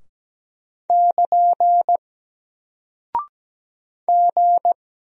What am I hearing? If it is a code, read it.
Morse code, heard as a keyed tone, about 17 wpm, 700 Hz. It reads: <KN> G
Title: Single Character (Letter:Number:Punctuation:Prosign) Alt 17wpm